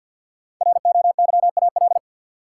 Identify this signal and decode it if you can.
Morse code, heard as a keyed tone, about 48 wpm, 700 Hz. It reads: ROYAL
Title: Five Letter Words - 50wpm